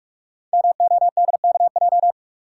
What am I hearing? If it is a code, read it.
Morse code, heard as a keyed tone, about 44 wpm, 700 Hz. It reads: MODKJ
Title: Code Groups - 5 Letter:Numbers - 45wpm